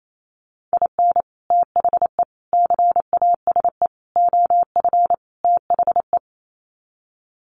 Morse code, heard as a keyed tone, about 28 wpm, 700 Hz. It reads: IN THE CASE OF THE
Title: Sets of 5 Words 28wpm